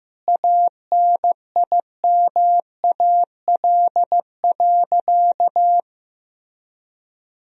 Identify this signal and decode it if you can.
Morse code, heard as a keyed tone, about 16 wpm, 700 Hz. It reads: ANIMAL.
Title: Sentences from Top 1000 Words - Review Part 1 - 15wpm